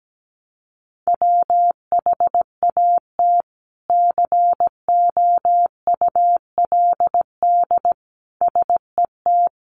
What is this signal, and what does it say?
Morse code, heard as a keyed tone, about 17 wpm, 700 Hz. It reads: WHAT COULD SET
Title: Sentences from Top 200 Words 17wpm